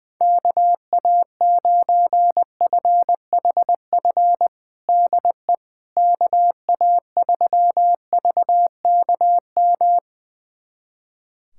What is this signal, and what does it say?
Morse code, heard as a keyed tone, about 20 wpm, 700 Hz. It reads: KA9FHF DE KA3VKM
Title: USA Callsign DE Callsign - 6 Characters - 20wpm